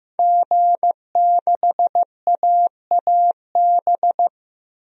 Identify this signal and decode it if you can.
Morse code, heard as a keyed tone, about 15 wpm, 700 Hz. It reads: G6AAB
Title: DXCC-100 Call Signs - 5 Characters - Repeated 2x - 15wpm